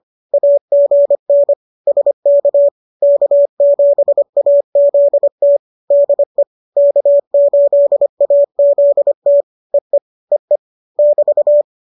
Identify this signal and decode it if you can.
Morse code, heard as a keyed tone, about 25 wpm, 570 Hz. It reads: AGN SK K7AZT DE K8AZT EE EE <BT>